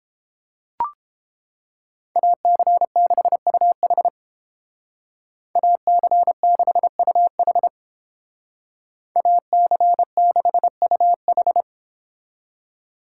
Morse code, heard as a keyed tone, about 30 wpm, 700 Hz. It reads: AC6UH AC6UH AC6UH